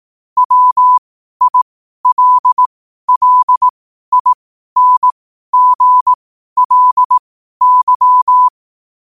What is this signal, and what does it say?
Morse code, heard as a keyed tone, about 18 wpm, spaced as for 12 wpm, 1000 Hz. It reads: WILLINGLY
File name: qotdfiles / 12wpm